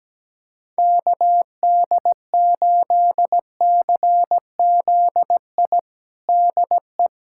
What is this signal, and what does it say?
Morse code, heard as a keyed tone, about 17 wpm, 700 Hz. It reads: KD8CZI DE
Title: USA Callsign DE Callsign 17wpm